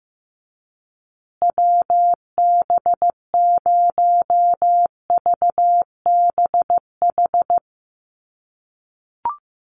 Morse code, heard as a keyed tone, about 15 wpm, 700 Hz. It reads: WB0VBH
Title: USA Call Signs Repeated 3x 15wpm